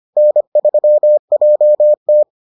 Morse code, heard as a keyed tone, about 25 wpm, 600 Hz. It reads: N3JT